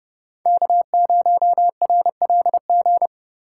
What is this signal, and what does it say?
Morse code, heard as a keyed tone, about 29 wpm, 700 Hz. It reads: K0RLG